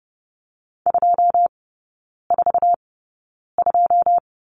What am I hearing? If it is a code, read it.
Morse code, heard as a keyed tone, about 30 wpm, spaced as for 9 wpm, 700 Hz. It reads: 242
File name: 081 Lesson 14e - Number 2 - Three Characters 30wpm